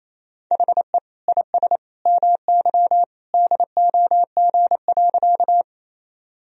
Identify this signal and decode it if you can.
Morse code, heard as a keyed tone, about 28 wpm, 700 Hz. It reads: HE IS MY DOG.
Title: Sentences from Top 1000 Words - Review Part 2 - 28wpm